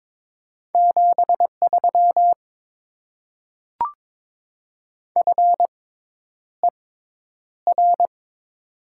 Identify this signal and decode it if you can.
Morse code, heard as a keyed tone, about 22 wpm, spaced as for 10 wpm, 700 Hz. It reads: 73 FER
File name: Top 20 QSO Elements - 6x Character Spacing - 22wpm